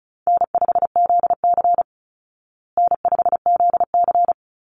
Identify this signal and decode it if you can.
Morse code, heard as a keyed tone, about 34 wpm, 700 Hz. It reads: N5ZC N5ZC
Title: Call Signs Repeated 3x - 35wpm